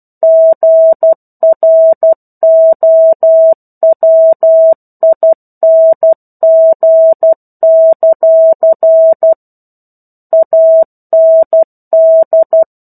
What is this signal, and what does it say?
Morse code, heard as a keyed tone, about 12 wpm, 650 Hz. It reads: GROWING; AND
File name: 01-Chapter-01 12 WPM